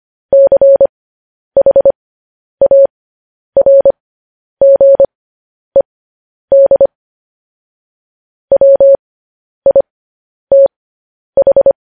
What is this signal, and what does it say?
Morse code, heard as a keyed tone, about 25 wpm, spaced as for 10 wpm, 550 Hz. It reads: CHARGED WITH
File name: podcast / News headlines-1x-25x10